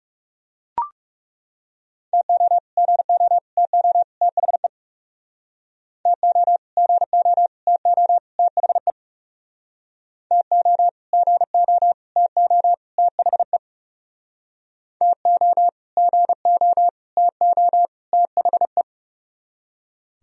Morse code, heard as a keyed tone, about 39 wpm, 700 Hz. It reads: TO GO TO THE TO GO TO THE TO GO TO THE TO GO TO THE